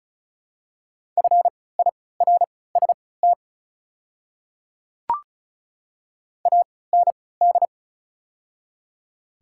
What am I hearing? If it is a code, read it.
Morse code, heard as a keyed tone, about 34 wpm, spaced as for 20 wpm, 690 Hz. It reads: FIRST AND
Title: 3x Character Spacing - Top 100 Words - 35wpm